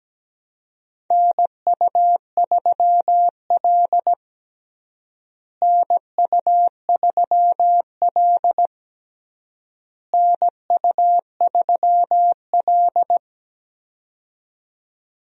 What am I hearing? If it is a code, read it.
Morse code, heard as a keyed tone, about 17 wpm, 700 Hz. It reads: NU3L NU3L NU3L
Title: USA Call Signs - 4 Characters - Repeated 3x - 17wpm